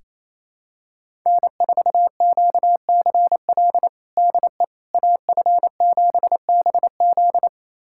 Morse code, heard as a keyed tone, about 27 wpm, 700 Hz. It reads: N4QCL DE AF7BZ
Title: USA Callsign DE Callsign - 5 Characters - 28wpm